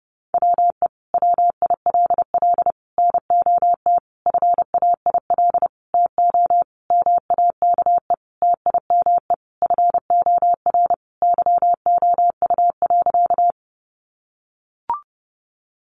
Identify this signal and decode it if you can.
Morse code, heard as a keyed tone, about 30 wpm, 700 Hz. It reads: WE WILL NOT FAIL TO MAKE TIME FOR YOU.
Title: Sentences from Top 1100 Words 30wpm